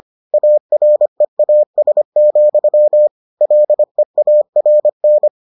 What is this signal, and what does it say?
Morse code, heard as a keyed tone, about 25 wpm, 600 Hz. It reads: AREAS, LEARN